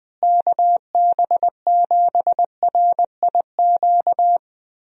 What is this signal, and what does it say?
Morse code, heard as a keyed tone, about 20 wpm, 700 Hz. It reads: KB7RIQ
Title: USA Call Signs - 6 Characters - Repeated 3x - 20wpm